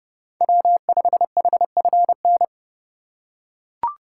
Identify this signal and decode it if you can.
Morse code, heard as a keyed tone, about 32 wpm, 700 Hz. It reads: W5HFN
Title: USA Call Signs - 5 Characters - 3x Character Spacing - 30wpm